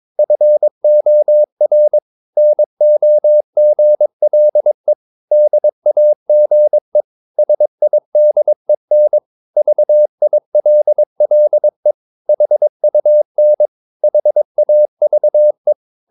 Morse code, heard as a keyed tone, about 22 wpm, 600 Hz. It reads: FOR NOGLE DAGE SIDEN VILLE HUN HAVE
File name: Ferie QRS 22wpm